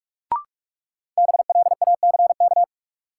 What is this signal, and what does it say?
Morse code, heard as a keyed tone, about 46 wpm, 700 Hz. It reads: BLACK
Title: Sentences from Top 400 Words - New Word Drill - 45wpm